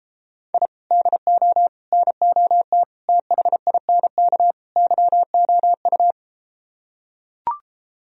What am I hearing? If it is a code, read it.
Morse code, heard as a keyed tone, about 33 wpm, 700 Hz. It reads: I DO NOT THINK YOU